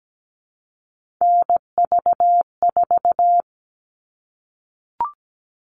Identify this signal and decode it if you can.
Morse code, heard as a keyed tone, about 17 wpm, 700 Hz. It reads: NV4
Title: All Cut Numbers - 3-Digit 17wpm